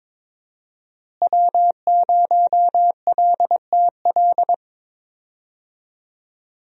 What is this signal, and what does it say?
Morse code, heard as a keyed tone, about 22 wpm, 700 Hz. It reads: W0LTL
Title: USA Call Signs - 5 Characters - Repeated 3x - 22wpm